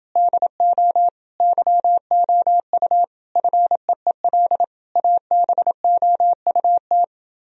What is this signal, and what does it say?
Morse code, heard as a keyed tone, about 30 wpm, 700 Hz. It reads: DO YOU FEEL ABOUT